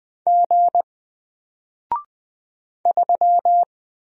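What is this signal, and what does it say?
Morse code, heard as a keyed tone, about 20 wpm, 700 Hz. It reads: G 3
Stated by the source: Single letter and number Morse drill at 20 wpm, from N4LCD.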